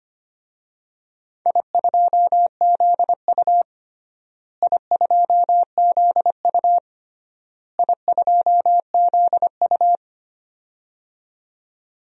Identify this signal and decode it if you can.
Morse code, heard as a keyed tone, about 25 wpm, 700 Hz. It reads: I2ZU I2ZU I2ZU
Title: DXCC-100 Call Signs - 4 Characters - Repeated 3x - 25wpm